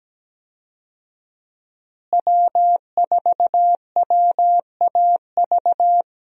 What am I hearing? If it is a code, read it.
Morse code, heard as a keyed tone, about 17 wpm, 700 Hz. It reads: W4WAV
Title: USA Call Signs - 5 Characters - Repeated 2x - 17wpm